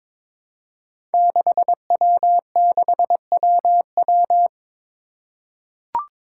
Morse code, heard as a keyed tone, about 22 wpm, 700 Hz. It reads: 6W6WW
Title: Five Challenging Letter-Number Combinations 22wpm